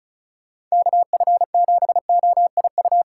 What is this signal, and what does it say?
Morse code, heard as a keyed tone, about 36 wpm, 700 Hz. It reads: KF7OIU
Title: USA Call Signs - 6 Characters - 2x Character Spacing - 35wpm